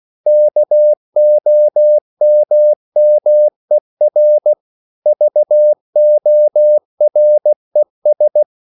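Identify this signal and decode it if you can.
Morse code, heard as a keyed tone, about 16 wpm, 600 Hz. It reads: KOMMER VORES
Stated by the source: Slow (QRS) Morse code at 16 wpm, from CW med Gnister.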